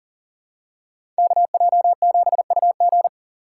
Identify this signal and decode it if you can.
Morse code, heard as a keyed tone, about 38 wpm, 700 Hz. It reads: KJ7UG
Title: USA Call Signs - 5 Characters - 3x Character Spacing - 40wpm